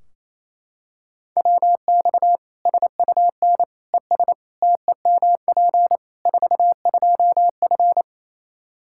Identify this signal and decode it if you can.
Morse code, heard as a keyed tone, about 28 wpm, 700 Hz. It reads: WX SUN ES TEMP 42F
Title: K1USN SST QSO Practice 28wpm